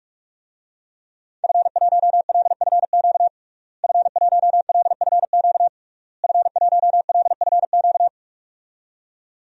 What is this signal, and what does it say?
Morse code, heard as a keyed tone, about 41 wpm, 700 Hz. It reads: F1LFQ F1LFQ F1LFQ
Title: DXCC-100 Call Signs - 5 Characters - Repeated 3x - 45wpm